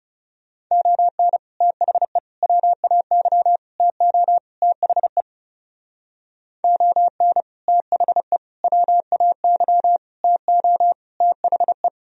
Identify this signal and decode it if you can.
Morse code, heard as a keyed tone, about 36 wpm, 700 Hz. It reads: ON THE WAY TO THE ON THE WAY TO THE
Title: Speed-Racing - Sets of 5 Words 30wpm